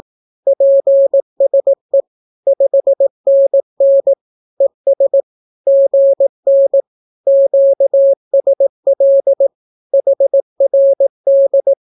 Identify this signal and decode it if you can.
Morse code, heard as a keyed tone, about 18 wpm, 550 Hz. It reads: PSE 5NN ES GN QSL HRD
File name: naukapodstaw / skroty / skroty